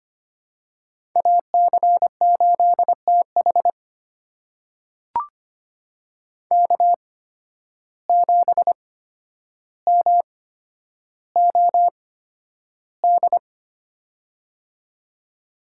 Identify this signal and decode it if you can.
Morse code, heard as a keyed tone, about 25 wpm, spaced as for 11 wpm, 700 Hz. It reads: AC8TH K7MOD
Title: USA Call Signs - 5 Characters - 8x Character Spacing - 25wpm